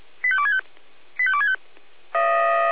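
Five-tone selective call (ZVEI), heard as quick runs of five tones, tones of about 70 ms. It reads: 85265 85265
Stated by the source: ZVEI five-tone selective call received by SDR.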